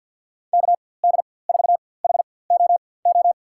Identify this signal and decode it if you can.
Morse code, heard as a keyed tone, about 49 wpm, spaced as for 23 wpm, 700 Hz. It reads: KD4HOO
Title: USA Call Signs - 6 Characters - 4x Character Spacing - 50wpm